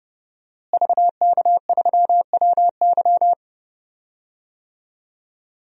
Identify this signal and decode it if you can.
Morse code, heard as a keyed tone, about 30 wpm, 700 Hz. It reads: VK3WY
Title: DXCC-100 Call Signs - 5 Characters - Repeated 2x - 30wpm